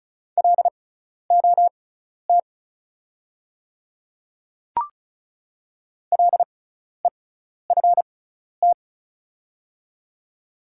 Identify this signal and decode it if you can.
Morse code, heard as a keyed tone, about 36 wpm, spaced as for 12 wpm, 690 Hz. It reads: LOT LEFT
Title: 6x Character Spacing - Top 500 Words - 35wpm